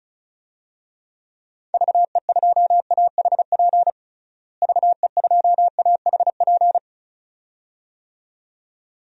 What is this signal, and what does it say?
Morse code, heard as a keyed tone, about 36 wpm, 700 Hz. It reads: VE2AHP VE2AHP
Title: DXCC-100 Call Signs - 6 Characters - Repeated 2x - 35wpm